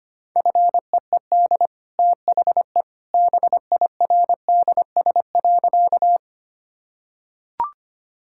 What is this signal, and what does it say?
Morse code, heard as a keyed tone, about 25 wpm, 700 Hz. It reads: FEED THE BIRDS.
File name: Easy Sentences with Hints - 25wpm